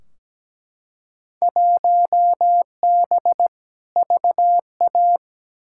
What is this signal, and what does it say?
Morse code, heard as a keyed tone, about 17 wpm, 700 Hz. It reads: 1B VA
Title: Field Day Exchanges - 17wpm